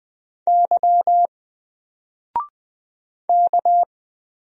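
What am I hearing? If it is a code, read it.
Morse code, heard as a keyed tone, about 20 wpm, 700 Hz. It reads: Y K
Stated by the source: Single letter and number Morse drill at 20 wpm, from N4LCD.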